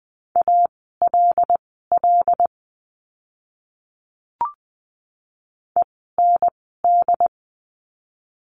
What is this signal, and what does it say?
Morse code, heard as a keyed tone, about 20 wpm, spaced as for 14 wpm, 700 Hz. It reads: ALL END
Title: 075 Lesson 13f - Letter L - Words 20wpm